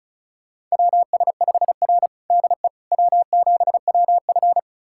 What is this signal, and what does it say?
Morse code, heard as a keyed tone, about 36 wpm, 700 Hz. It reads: WS5R DE W7WF
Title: USA Callsign DE Callsign - 4 Characters - 35wpm